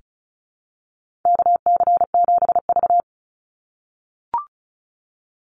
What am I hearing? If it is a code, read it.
Morse code, heard as a keyed tone, about 34 wpm, 700 Hz. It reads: KC7V
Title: Call Signs - 35wpm